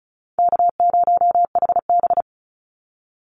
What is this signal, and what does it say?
Morse code, heard as a keyed tone, about 34 wpm, 700 Hz. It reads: K0HB
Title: Call Signs - 35wpm